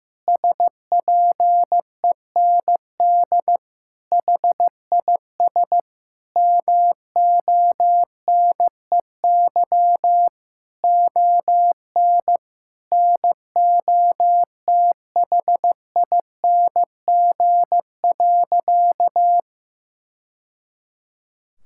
Morse code, heard as a keyed tone, about 16 wpm, 700 Hz. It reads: SPEND HIS MONEY ON NOTHING.
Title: Sentences from Top 1000 Words - Review Part 5 - 15wpm